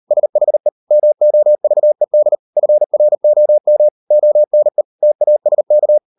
Morse code, heard as a keyed tone, about 40 wpm, 600 Hz. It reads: SHE MOVED FROM ONE TASK